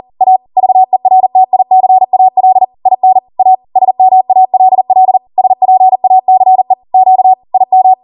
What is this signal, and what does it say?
Morse code, heard as a keyed tone, about 40 wpm, 750 Hz. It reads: A VERTICAL IN A SMALL SPACE Q IM
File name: Morse / 40WPM